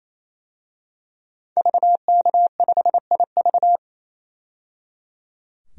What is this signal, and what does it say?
Morse code, heard as a keyed tone, about 27 wpm, 700 Hz. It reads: VK5IV
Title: DXCC-100 Call Signs - 5 Characters - 28wpm